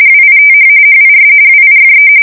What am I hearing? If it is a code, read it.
RTTY, about 45.45 baud. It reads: WITH HIS WIFE
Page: 170 Shift 2125/2295